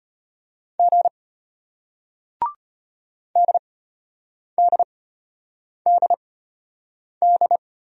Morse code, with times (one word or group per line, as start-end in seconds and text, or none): 0.8-1.1 G
3.4-7.6 DDDD